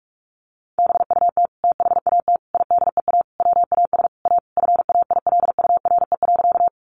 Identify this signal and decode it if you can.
Morse code, heard as a keyed tone, about 44 wpm, 700 Hz. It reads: BUT THAT IDEA WAS A FAILURE.
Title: Sentences from Top 900 Words 45wpm